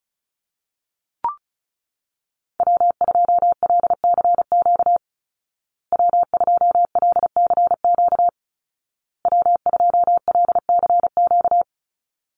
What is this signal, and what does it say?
Morse code, heard as a keyed tone, about 34 wpm, 700 Hz. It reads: W2LCQ W2LCQ W2LCQ
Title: Call Signs Repeated 2x - 35wpm